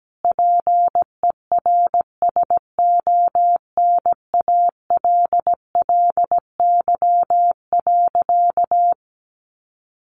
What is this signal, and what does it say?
Morse code, heard as a keyed tone, about 17 wpm, 700 Hz. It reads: PERSONALLY.
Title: Sentences from Top 1400 Words 17wpm